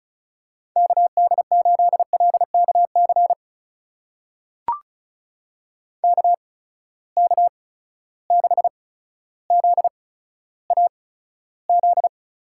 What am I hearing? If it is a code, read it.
Morse code, heard as a keyed tone, about 36 wpm, spaced as for 15 wpm, 700 Hz. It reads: KD8LKC KK6ZAZ
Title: USA Call Signs - 6 Characters - 8x Character Spacing - 35wpm